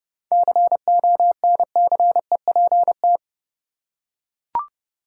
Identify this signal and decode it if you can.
Morse code, heard as a keyed tone, about 30 wpm, 700 Hz. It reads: CONCEPT
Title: Seven Letter Words - 30wpm